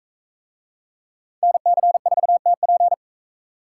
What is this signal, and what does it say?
Morse code, heard as a keyed tone, about 41 wpm, 700 Hz. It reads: NC4TP